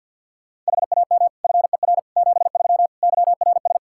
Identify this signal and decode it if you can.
Morse code, heard as a keyed tone, about 48 wpm, 700 Hz. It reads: HAM FER 73 YRS
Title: Practice QSO - Exchange 3 - More Info - 50wpm